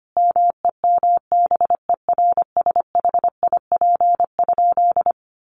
Morse code, heard as a keyed tone, about 25 wpm, 700 Hz. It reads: MEMBERSHIP?
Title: Sentences from Top 1400 Words 25wpm